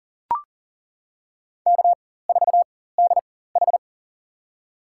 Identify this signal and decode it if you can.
Morse code, heard as a keyed tone, about 40 wpm, spaced as for 19 wpm, 700 Hz. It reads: K4DH
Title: USA Call Signs - 4 Characters - 4x Character Spacing - 40wpm